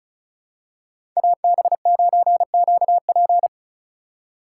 Morse code, heard as a keyed tone, about 34 wpm, 700 Hz. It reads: AB9QP
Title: USA Call Signs - 5 Characters - 4x Character Spacing - 35wpm